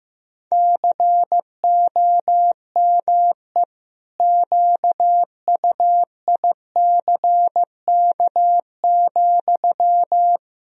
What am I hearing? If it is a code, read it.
Morse code, heard as a keyed tone, about 15 wpm, 700 Hz. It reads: COME QUICK,
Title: Sentences from Top 1000 Words - Review Part 3 - 15wpm